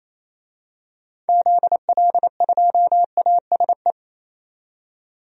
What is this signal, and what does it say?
Morse code, heard as a keyed tone, about 28 wpm, 700 Hz. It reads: ZL2ASE
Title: DXCC-100 Call Signs - 6 Characters - 28wpm